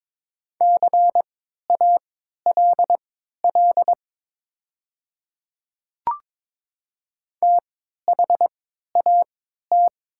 Morse code, heard as a keyed tone, about 22 wpm, spaced as for 12 wpm, 690 Hz. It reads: CALL THAT
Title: 3x Character Spacing - Top 200 Words - 22wpm